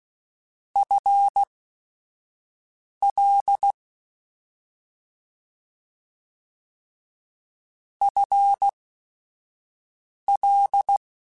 Morse code, heard as a keyed tone, about 16 wpm, spaced as for 5 wpm, 800 Hz. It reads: FL FL
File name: Short Runs / Short Runs-mp3s / ShortContrasts @5